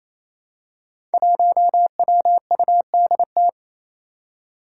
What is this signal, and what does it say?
Morse code, heard as a keyed tone, about 27 wpm, 700 Hz. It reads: 1WUDT
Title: Five Challenging Letter-Number Combinations 28wpm